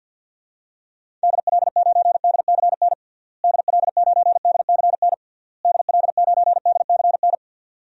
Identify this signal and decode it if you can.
Morse code, heard as a keyed tone, about 52 wpm, 700 Hz. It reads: DL9DCN DL9DCN DL9DCN
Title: DXCC-100 Call Signs - 6 Characters - Repeated 3x - 50wpm